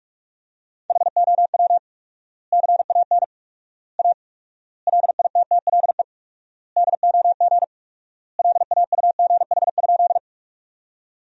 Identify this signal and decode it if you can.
Morse code, heard as a keyed tone, about 43 wpm, 700 Hz. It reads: HOW CAN A LITTLE DOG LAUGH?
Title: 4x Word-Spacing - Sentences from Top 500 Words - 45wpm